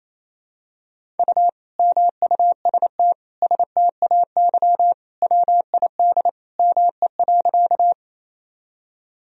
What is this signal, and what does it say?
Morse code, heard as a keyed tone, about 28 wpm, 700 Hz. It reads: U MUST STAY WID ME.
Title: Sentences from Top 500 Words and Abbreviations 28wpm